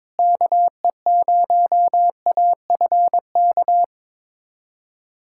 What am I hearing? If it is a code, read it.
Morse code, heard as a keyed tone, about 22 wpm, 700 Hz. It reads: KE0AFK
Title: USA Callsign DE Callsign 22wpm